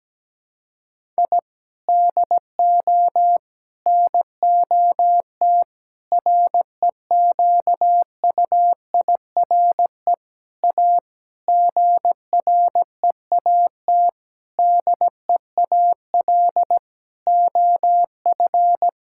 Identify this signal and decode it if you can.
Morse code, heard as a keyed tone, about 17 wpm, 700 Hz. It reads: I DO NOT REQUIRE A GREAT DEAL OF
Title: Sentences from Top 1000 Words - Review Part 6 - 17wpm